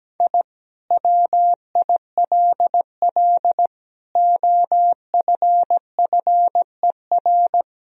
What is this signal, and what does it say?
Morse code, heard as a keyed tone, about 17 wpm, 700 Hz. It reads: I WILL OFFER
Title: Sentences from Top 1000 Words - Review Part 4 - 17wpm